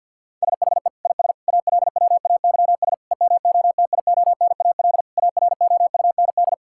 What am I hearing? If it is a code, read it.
Morse code, heard as a keyed tone, about 47 wpm, 690 Hz. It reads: SHE IS ALWAYS EMOTIONAL AROUND
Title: Sentences from Top 1000 Words - Review Part 5 - 50wpm